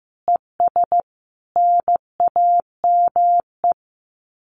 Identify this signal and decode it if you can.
Morse code, heard as a keyed tone, about 15 wpm, 700 Hz. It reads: ES NAME
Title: Common QSO Phrases 15wpm